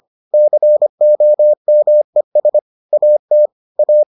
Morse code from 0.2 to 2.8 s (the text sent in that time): COMES